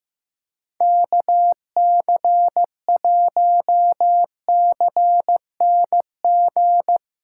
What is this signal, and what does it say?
Morse code, heard as a keyed tone, about 15 wpm, 700 Hz. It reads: KC1CNG
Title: USA Callsign DE Callsign 15wpm